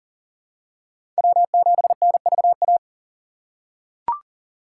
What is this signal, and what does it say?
Morse code, heard as a keyed tone, about 42 wpm, 700 Hz. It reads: W7NVA
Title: USA Call Signs - 5 Characters - 3x Character Spacing - 40wpm